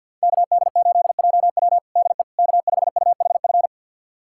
Morse code, heard as a keyed tone, about 52 wpm, 700 Hz. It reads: KD8JW DE K5UHF
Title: USA Callsign DE Callsign - 5 Characters - 50wpm